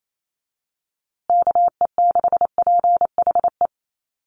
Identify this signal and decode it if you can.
Morse code, heard as a keyed tone, about 28 wpm, 700 Hz. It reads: KE6PHE
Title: USA Call Signs Repeated 3x 28wpm